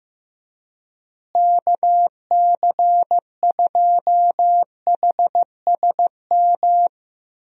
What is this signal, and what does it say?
Morse code, heard as a keyed tone, about 15 wpm, 700 Hz. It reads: KC2HSM
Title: USA Call Signs - 6 Characters - Repeated 2x - 15wpm